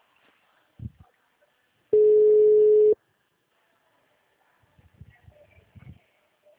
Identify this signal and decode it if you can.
Ringback tone (Europe, 425 Hz).